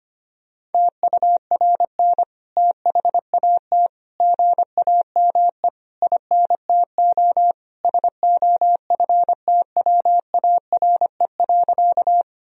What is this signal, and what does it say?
Morse code, heard as a keyed tone, about 25 wpm, 700 Hz. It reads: TURN THAT GAME INTO SOFTWARE.
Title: Sentences from Top 1000 Words - Review Part 3 - 25wpm